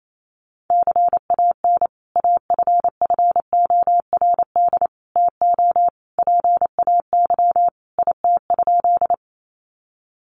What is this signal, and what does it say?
Morse code, heard as a keyed tone, about 28 wpm, 700 Hz. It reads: CAN AFFORD TO PAY IT?
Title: Sentences from Top 1100 Words 28wpm